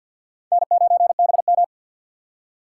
Morse code, heard as a keyed tone, about 47 wpm, 700 Hz. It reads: N9BM
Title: USA Call Signs - 4 Characters - 6x Character Spacing - 50wpm